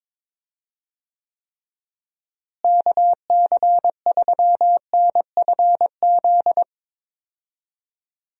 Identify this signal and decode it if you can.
Morse code, heard as a keyed tone, about 22 wpm, 700 Hz. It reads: KC3NFZ
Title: USA Call Signs - 6 Characters - Repeated 3x - 22wpm